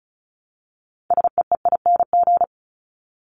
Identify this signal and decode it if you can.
Morse code, heard as a keyed tone, about 34 wpm, 700 Hz. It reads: SEEING
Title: ING Suffix - Top 100 Words with Base Words 35wpm